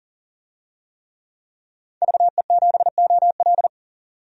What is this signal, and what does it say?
Morse code, heard as a keyed tone, about 40 wpm, 700 Hz. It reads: VE7OL
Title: DXCC-100 Call Signs - 5 Characters - 40wpm